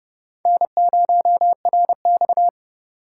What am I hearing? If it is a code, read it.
Morse code, heard as a keyed tone, about 38 wpm, 700 Hz. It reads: N0RX